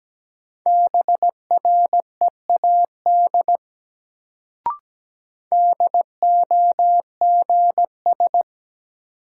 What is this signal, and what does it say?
Morse code, heard as a keyed tone, about 17 wpm, 700 Hz. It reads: BREAD DOGS
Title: Sentences from Top 1600 Words - New Word Drill - 17wpm